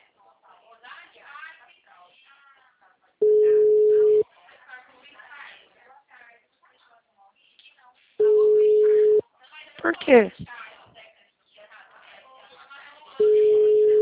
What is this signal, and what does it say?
Ringback tone (Europe, 425 Hz), with speech.
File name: out- -2022